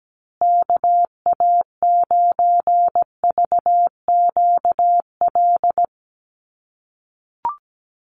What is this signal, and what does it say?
Morse code, heard as a keyed tone, about 17 wpm, 700 Hz. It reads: KA9VQL
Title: USA Call Signs Repeated 3x 17wpm